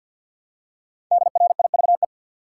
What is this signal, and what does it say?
Morse code, heard as a keyed tone, about 53 wpm, 700 Hz. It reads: DRIVE
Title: Five Letter Words - 50wpm